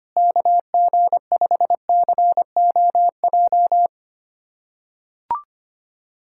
Morse code, heard as a keyed tone, about 25 wpm, 700 Hz. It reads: KG5COJ
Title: USA Callsign DE Callsign - 6 Characters - 25wpm